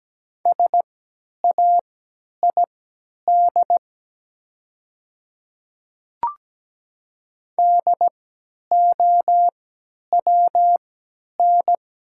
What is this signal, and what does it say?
Morse code, heard as a keyed tone, about 17 wpm, spaced as for 10 wpm, 700 Hz. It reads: SAID DOWN
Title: 3x Character Spacing - Top 100 Words - 17wpm